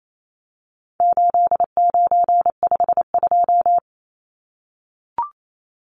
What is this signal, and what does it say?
Morse code, heard as a keyed tone, about 28 wpm, 700 Hz. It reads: 8952
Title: Numbers - 4-Digit 28wpm